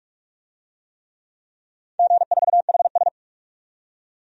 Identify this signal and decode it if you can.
Morse code, heard as a keyed tone, about 43 wpm, 700 Hz. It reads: G4HS
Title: DXCC-100 Call Signs - 4 Characters - 45wpm